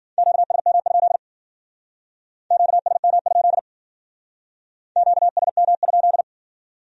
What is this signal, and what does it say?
Morse code, heard as a keyed tone, about 48 wpm, 700 Hz. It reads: QSM? QSM? QSM?
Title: Speed-Racing Q-Codes - 40wpm